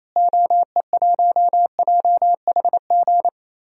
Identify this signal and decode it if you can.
Morse code, heard as a keyed tone, about 27 wpm, 700 Hz. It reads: OE1JHG
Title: DXCC-100 Call Signs - 6 Characters - Repeated 3x - 28wpm